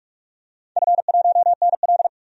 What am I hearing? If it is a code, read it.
Morse code, heard as a keyed tone, about 46 wpm, 700 Hz. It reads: F1NL
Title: DXCC-100 Call Signs - 4 Characters - 45wpm